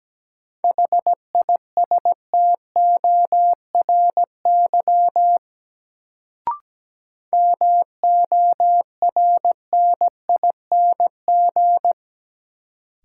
Morse code, heard as a keyed tone, about 17 wpm, 700 Hz. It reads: HISTORY MORNING
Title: Sets of 3 Words - New Word Drill - 17wpm